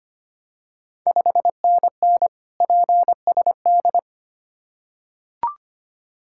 Morse code, heard as a keyed tone, about 25 wpm, 690 Hz. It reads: 5NN PSD
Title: Ontario QSO Party Exchange 25wpm